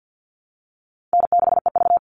Morse code, heard as a keyed tone, about 50 wpm, 700 Hz. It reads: N6EV
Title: Easy Call Signs - 50wpm